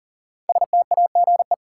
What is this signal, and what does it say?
Morse code, heard as a keyed tone, about 40 wpm, 700 Hz. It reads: STAGE
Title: Sentences from Top 1200 Words - New Word Drill Alt - 40wpm